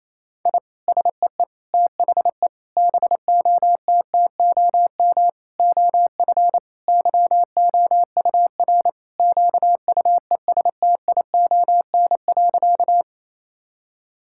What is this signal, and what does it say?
Morse code, heard as a keyed tone, about 28 wpm, 700 Hz. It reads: I SEE THE BOTTOM OF YOUR QUESTION.
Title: Sentences from Top 1000 Words - Review Part 6 - 28wpm